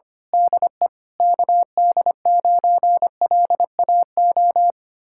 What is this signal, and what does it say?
Morse code, heard as a keyed tone, about 24 wpm, 700 Hz. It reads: DE KD9LAO
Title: Speed-Racing Common QSO Phrases Encore - 25wpm